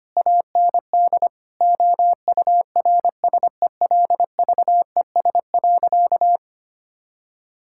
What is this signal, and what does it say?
Morse code, heard as a keyed tone, about 25 wpm, 700 Hz. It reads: AND OURSELVES.
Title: Sentences from Top 1600 Words 25wpm